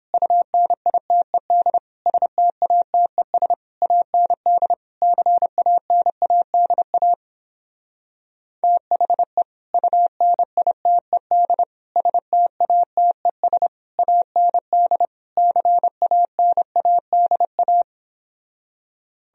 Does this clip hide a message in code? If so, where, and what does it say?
Yes: Morse code from 0.1 to 17.8 s, heard as a keyed tone, about 29 wpm, 700 Hz. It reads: UNITED STATES AND CANADA THE UNITED STATES AND CANADA